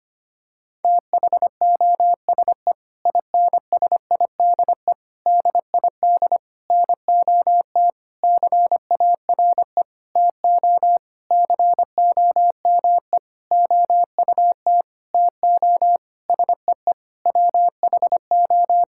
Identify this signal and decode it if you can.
Morse code, heard as a keyed tone, about 25 wpm, 700 Hz. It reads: THOSE INSIDE DID NOT CARE TO COME OUT TO SEE WHO
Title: Sentences from Top 1000 Words - Review Part 6 - 25wpm